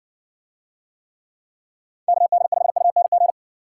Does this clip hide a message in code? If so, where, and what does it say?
Yes: Morse code from 2.1 to 3.3 s, heard as a keyed tone, about 60 wpm, 700 Hz. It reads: KD5UNG